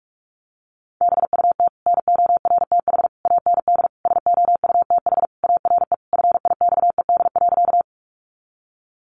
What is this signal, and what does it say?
Morse code, heard as a keyed tone, about 44 wpm, 700 Hz. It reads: BUT NORTH AND SOUTH ARE FIXED.
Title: Sentences from Top 1100 Words 45wpm